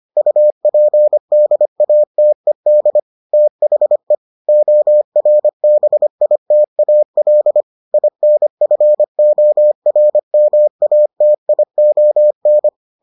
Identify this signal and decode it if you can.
Morse code, heard as a keyed tone, about 25 wpm, 600 Hz. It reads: UPDATED THE ORBITAL INFORMATION